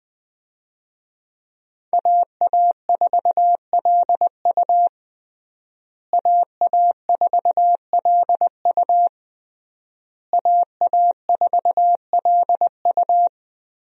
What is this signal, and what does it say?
Morse code, heard as a keyed tone, about 20 wpm, 700 Hz. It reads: AA4LU AA4LU AA4LU
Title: USA Call Signs - 5 Characters - Repeated 3x - 20wpm